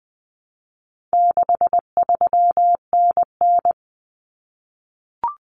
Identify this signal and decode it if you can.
Morse code, heard as a keyed tone, about 20 wpm, 700 Hz. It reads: 63NN
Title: Common Cut Numbers - 4-Digit 20wpm